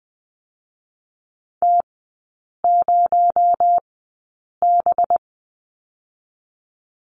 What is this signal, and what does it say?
Morse code, heard as a keyed tone, about 20 wpm, spaced as for 8 wpm, 700 Hz. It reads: T0B
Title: 218 Lesson 35e - Number 0 - Three Characters 20wpm